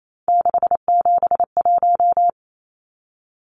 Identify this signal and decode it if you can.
Morse code, heard as a keyed tone, about 28 wpm, 700 Hz. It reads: 671
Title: Numbers - 3-Digit 28wpm